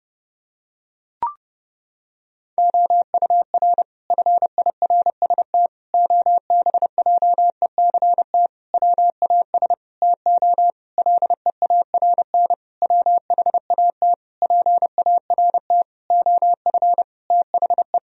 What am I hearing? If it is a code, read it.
Morse code, heard as a keyed tone, about 30 wpm, 700 Hz. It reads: OUR FIRST OBJECT WAS TO LEARN WHAT PART OF THE
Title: Sentences from Top 1000 Words - Review Part 3 - 30wpm